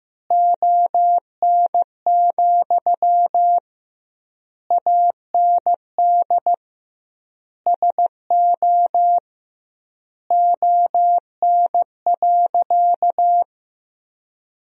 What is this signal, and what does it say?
Morse code, heard as a keyed tone, about 15 wpm, 700 Hz. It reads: ON, AND SO ON.
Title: 2x Word-Spacing - Sentences from Top 100 Words - 15wpm